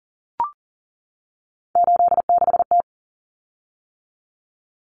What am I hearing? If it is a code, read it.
Morse code, heard as a keyed tone, about 40 wpm, 700 Hz. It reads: 86T
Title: Common Cut Numbers - 3-Digit 40wpm